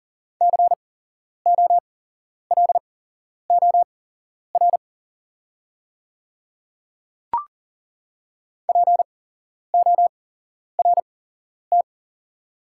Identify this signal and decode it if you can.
Morse code, heard as a keyed tone, about 40 wpm, spaced as for 11 wpm, 690 Hz. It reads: COLOR PORT
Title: 8x Character Spacing - Top 300 Words - 40wpm